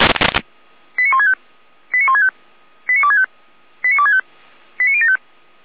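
Five-tone selective call (ZVEI), heard as quick runs of five tones, tones of about 80 ms. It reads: 89165 89165 89265 89265 89075
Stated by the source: ZVEI five-tone selective call 89170 received by SDR.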